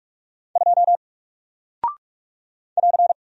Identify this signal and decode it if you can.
Morse code, heard as a keyed tone, about 48 wpm, 700 Hz. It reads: 2 <AR>
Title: Single Character (Letter:Number:Punctuation:Prosign) Alt 45wpm